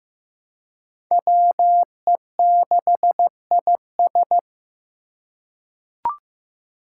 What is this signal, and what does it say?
Morse code, heard as a keyed tone, about 15 wpm, 700 Hz. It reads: WE6IS
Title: Five Challenging Letter-Number Combinations 15wpm